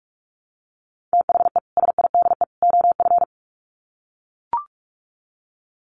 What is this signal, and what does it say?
Morse code, heard as a keyed tone, about 44 wpm, 700 Hz. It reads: THE SIDE OF
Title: Sets of 3 Words 45wpm